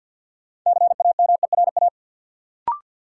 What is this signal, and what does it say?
Morse code, heard as a keyed tone, about 49 wpm, 700 Hz. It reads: CAMERA
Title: Sentences from Top 1000 Words - Review Part 5 - New Word Drill - 50wpm